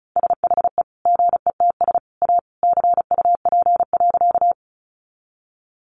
Morse code, heard as a keyed tone, about 34 wpm, 700 Hz. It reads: SHE GETS A CUP.
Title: Additional Easy Phrases 35wpm